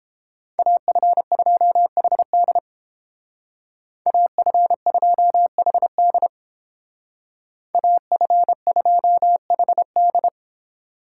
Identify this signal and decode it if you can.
Morse code, heard as a keyed tone, about 30 wpm, 700 Hz. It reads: AF2HD AF2HD AF2HD